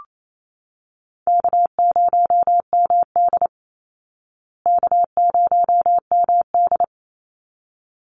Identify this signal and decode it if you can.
Morse code, heard as a keyed tone, about 28 wpm, 700 Hz. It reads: K0MD K0MD
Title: Call Signs Repeated 3x - 28wpm